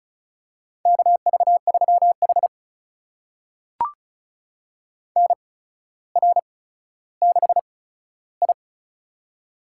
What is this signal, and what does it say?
Morse code, heard as a keyed tone, about 36 wpm, spaced as for 15 wpm, 700 Hz. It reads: KV3H NR6I
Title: USA Call Signs - 4 Characters - 8x Character Spacing - 35wpm